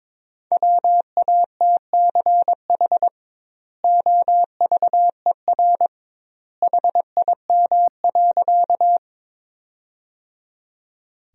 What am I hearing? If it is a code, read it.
Morse code, heard as a keyed tone, about 22 wpm, 700 Hz. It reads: WATCH OVER HIM.
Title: 2x Word-Spacing - Sentences from Top 300 Words - 22wpm